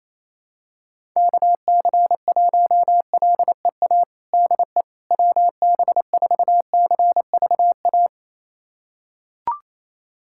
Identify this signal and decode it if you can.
Morse code, heard as a keyed tone, about 28 wpm, 700 Hz. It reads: KC1LEA DE WB4CVA
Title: USA Callsign DE Callsign 28wpm